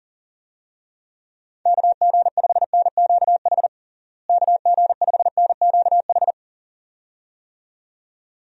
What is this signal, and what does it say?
Morse code, heard as a keyed tone, about 40 wpm, 700 Hz. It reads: KG5NQH KG5NQH
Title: USA Call Signs - 6 Characters - Repeated 2x - 40wpm